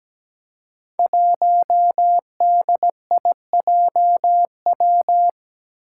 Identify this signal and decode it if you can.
Morse code, heard as a keyed tone, about 17 wpm, 700 Hz. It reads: 1DIJW
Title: Five Challenging Letter-Number Combinations 17wpm